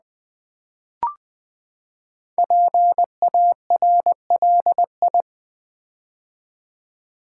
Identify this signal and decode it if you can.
Morse code, heard as a keyed tone, about 20 wpm, 700 Hz. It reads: PARLI
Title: Build A Word - 5 Letters At A Time - 10-Letter Words - 20wpm